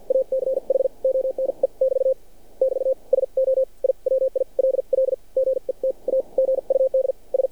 Morse code, heard as a keyed tone, about 51 wpm, spaced as for 42 wpm, 530 Hz. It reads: ACH ONE <BT> <BT> SO I WILL GET A GUD S
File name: archived / CRSnet